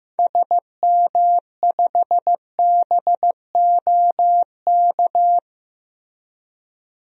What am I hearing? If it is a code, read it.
Morse code, heard as a keyed tone, about 16 wpm, 700 Hz. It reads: SM5BOK
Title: DXCC-100 Call Signs - 6 Characters - Repeated 3x - 15wpm